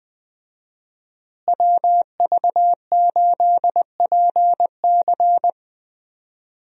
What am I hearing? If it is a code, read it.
Morse code, heard as a keyed tone, about 20 wpm, 700 Hz. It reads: WV8PC
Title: USA Call Signs - 5 Characters - Repeated 3x - 20wpm